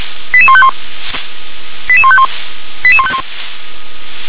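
Five-tone selective call (ZVEI), heard as quick runs of five tones, tones of about 70 ms. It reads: 88151 88151 88151
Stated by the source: ZVEI five-tone call received by SDR.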